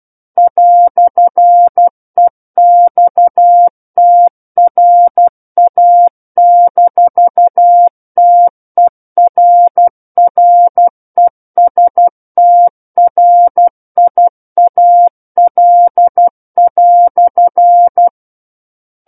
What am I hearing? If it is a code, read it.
Morse code, heard as a keyed tone, about 12 wpm, 700 Hz. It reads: "EXTRA-TERRESTRIAL"